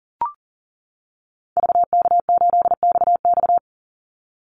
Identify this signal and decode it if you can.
Morse code, heard as a keyed tone, about 40 wpm, 700 Hz. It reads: VK8XX
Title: Call Signs - 40wpm